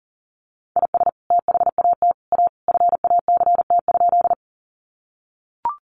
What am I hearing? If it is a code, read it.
Morse code, heard as a keyed tone, about 40 wpm, 700 Hz. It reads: IS THAT A FACT?
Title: Sentences from Top 500 Words 40wpm